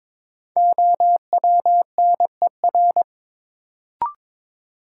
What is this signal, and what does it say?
Morse code, heard as a keyed tone, about 22 wpm, 700 Hz. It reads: OWNER
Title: Sentences from Top 1600 Words - New Word Drill - 22wpm